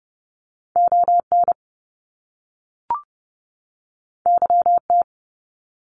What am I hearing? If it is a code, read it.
Morse code, heard as a keyed tone, about 30 wpm, 700 Hz. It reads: ON YT
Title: Canadian Provinces and Territories 30wpm